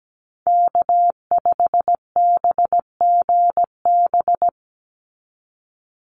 Morse code, heard as a keyed tone, about 17 wpm, 700 Hz. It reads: K5BGB
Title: Call Signs Repeated 3x - 17wpm